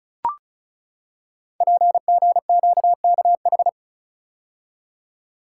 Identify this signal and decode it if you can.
Morse code, heard as a keyed tone, about 36 wpm, 700 Hz. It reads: PGQKH
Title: Code Groups - 5 Letters - 35wpm